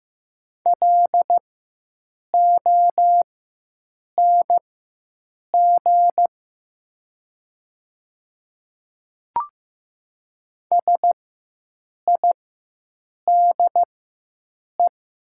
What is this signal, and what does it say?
Morse code, heard as a keyed tone, about 15 wpm, spaced as for 7 wpm, 700 Hz. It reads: LONG SIDE
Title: 4x Character Spacing - Top 100 Words - 15wpm